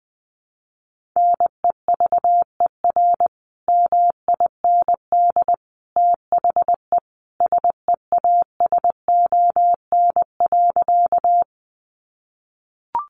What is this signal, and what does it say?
Morse code, heard as a keyed tone, about 20 wpm, 700 Hz. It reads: NEVER MIND THE SEASON.
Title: Sentences from Top 1400 Words 20wpm